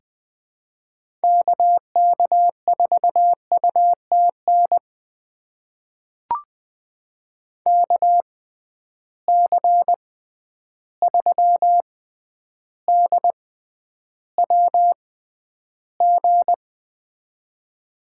Morse code, heard as a keyed tone, about 20 wpm, spaced as for 10 wpm, 700 Hz. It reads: KK4UTN KC3DWG
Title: USA Call Signs - 6 Characters - 6x Character Spacing - 20wpm